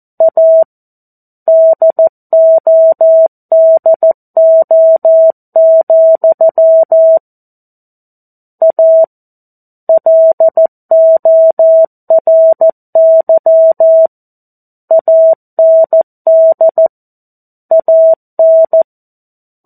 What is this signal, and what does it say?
Morse code, heard as a keyed tone, about 14 wpm, 650 Hz. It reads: A DODO, A LORY AND AN
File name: 02-Chapter-02 14 WPM